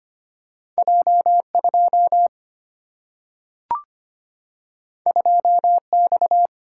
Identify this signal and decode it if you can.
Morse code, heard as a keyed tone, about 25 wpm, 700 Hz. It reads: J 2 2 X
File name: Two Letter-Number Combinations - 25wpm